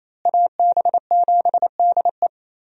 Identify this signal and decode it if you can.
Morse code, heard as a keyed tone, about 27 wpm, 700 Hz. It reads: AB7DE
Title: USA Call Signs - 5 Characters - Repeated 3x - 28wpm